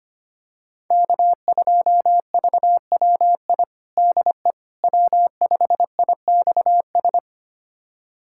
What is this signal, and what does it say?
Morse code, heard as a keyed tone, about 25 wpm, 700 Hz. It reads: K2VWI DE W5IXS
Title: USA Callsign DE Callsign - 5 Characters - 25wpm